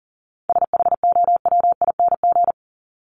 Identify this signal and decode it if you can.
Morse code, heard as a keyed tone, about 40 wpm, 700 Hz. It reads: SHOWING
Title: ING Suffix - Top 100 Words with Base Words 40wpm